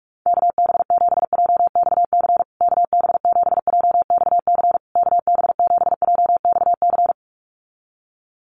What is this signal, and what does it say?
Morse code, heard as a keyed tone, about 44 wpm, 690 Hz. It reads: KB7JXC KB7JXC KB7JXC
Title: USA Call Signs Repeated 3x 45wpm